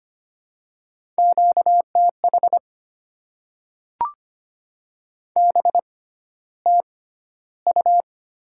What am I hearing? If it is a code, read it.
Morse code, heard as a keyed tone, about 25 wpm, spaced as for 13 wpm, 700 Hz. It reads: QTH BTU
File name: Top 90 QSO Elements - 6x Character Spacing - 25wpm